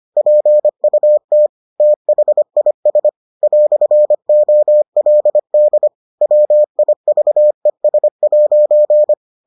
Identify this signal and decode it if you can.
Morse code, heard as a keyed tone, about 25 wpm, 600 Hz. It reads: PUT THIS "OLD WIVES'